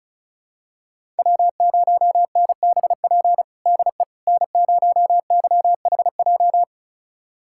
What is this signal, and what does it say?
Morse code, heard as a keyed tone, about 36 wpm, 700 Hz. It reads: W0NBP DE N0YHJ
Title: USA Callsign DE Callsign - 5 Characters - 35wpm